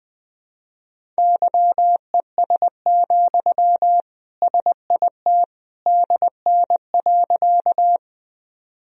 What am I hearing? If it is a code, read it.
Morse code, heard as a keyed tone, about 20 wpm, 700 Hz. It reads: YES, SIT DN.
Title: Sentences from Top 500 Words and Abbreviations 20wpm